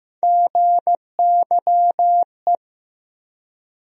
Morse code, heard as a keyed tone, about 15 wpm, 700 Hz. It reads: GYE
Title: USA Call Sign Postfixes - 3 Letters 15wpm